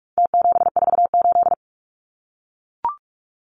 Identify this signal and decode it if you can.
Morse code, heard as a keyed tone, about 44 wpm, 700 Hz. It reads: T748
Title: Numbers - Contest Style - 4-Digit 45wpm